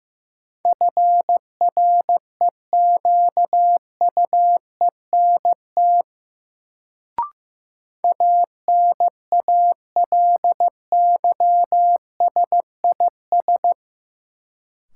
Morse code, heard as a keyed tone, about 15 wpm, 700 Hz. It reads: FREQUENT ANALYSIS
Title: Sentences from Top 1400 Words - New Word Drill Alt - 15wpm